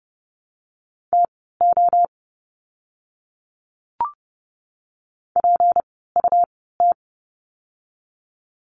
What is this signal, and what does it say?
Morse code, heard as a keyed tone, about 30 wpm, spaced as for 17 wpm, 700 Hz. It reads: TO PUT
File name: course / 30:10wpm / 237 Lesson 38f - Letter q - Words 30wpm